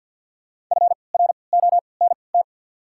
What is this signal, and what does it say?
Morse code, heard as a keyed tone, about 53 wpm, spaced as for 29 wpm, 700 Hz. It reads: FRONT
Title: 3x Character Spacing - Top 500 Words - 50wpm